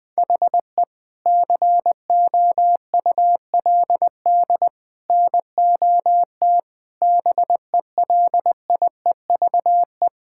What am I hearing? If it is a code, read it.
Morse code, heard as a keyed tone, about 21 wpm, 700 Hz. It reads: HE COULD NOT BELIEVE
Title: Sentences with 3-Word Phrases - 20wpm